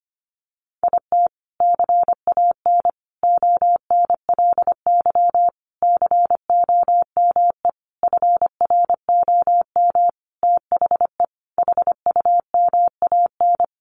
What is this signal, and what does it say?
Morse code, heard as a keyed tone, about 25 wpm, 700 Hz. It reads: IT CAN ONLY COME FROM THE HUMAN
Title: Sentences from Top 1400 Words 25wpm